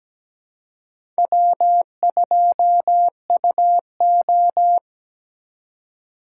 Morse code, heard as a keyed tone, about 17 wpm, 700 Hz. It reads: W2UO
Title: USA Call Signs - 4 Characters - 4x Character Spacing - 17wpm